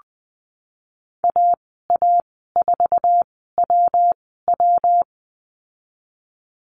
Morse code, heard as a keyed tone, about 20 wpm, spaced as for 15 wpm, 700 Hz. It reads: AA4WW
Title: 122 Lesson 20g - Number 3 - Call Signs 20wpm